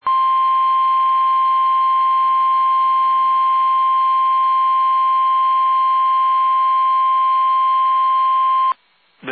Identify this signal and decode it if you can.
Emergency Alert System (SAME) tones (the 1050 Hz attention tone). It is heard from 0.0 to 8.8 s.
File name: eas / KUSO / KUSO